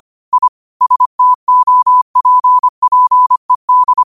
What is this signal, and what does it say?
Morse code, heard as a keyed tone, about 25 wpm, 1000 Hz. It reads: I STOPPED